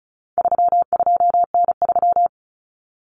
Morse code, heard as a keyed tone, about 34 wpm, 700 Hz. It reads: 32N3
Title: Common Cut Numbers - 4-Digit 35wpm